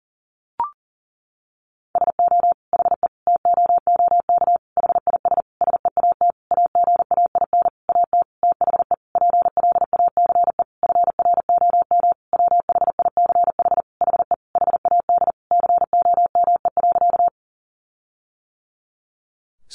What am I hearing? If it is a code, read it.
Morse code, heard as a keyed tone, about 40 wpm, 700 Hz. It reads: SO HE TOOK HIS SEAT AGAIN AT THE PLACE FROM WHICH HE HAD COME.
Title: Sentences from Top 1500 Words 40wpm